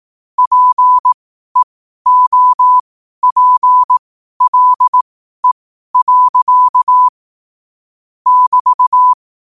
Morse code, heard as a keyed tone, about 18 wpm, spaced as for 12 wpm, 1000 Hz. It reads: PEOPLE. <BT>
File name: qotdfiles / 12wpm